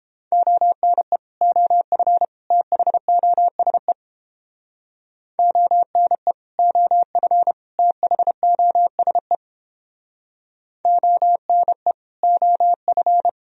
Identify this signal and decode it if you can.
Morse code, heard as a keyed tone, about 31 wpm, 700 Hz. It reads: ONE OF THOSE ONE OF THOSE ONE OF